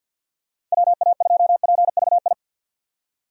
Morse code, heard as a keyed tone, about 49 wpm, 700 Hz. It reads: WA2PVI
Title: USA Call Signs - 6 Characters - 50wpm